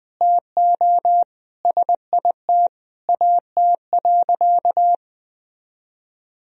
Morse code, heard as a keyed tone, about 21 wpm, 700 Hz. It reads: TO SIT AT.
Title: Sentences from Top 1000 Words - Review Part 2 - 20wpm